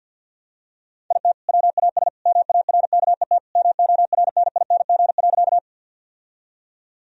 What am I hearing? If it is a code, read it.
Morse code, heard as a keyed tone, about 50 wpm, 700 Hz. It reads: IT WAS MARKET MORNING.
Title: Sentences from Top 1000 Words - Review Part 3 - 50wpm